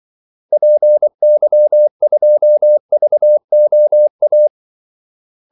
Morse code, heard as a keyed tone, about 24 wpm, 600 Hz. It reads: PY2VOA
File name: Callsigns-24wpm